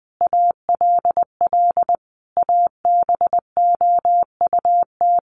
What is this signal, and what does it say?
Morse code, heard as a keyed tone, about 20 wpm, 700 Hz. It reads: ALL ABOUT
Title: Sentences from Top 200 Words 20wpm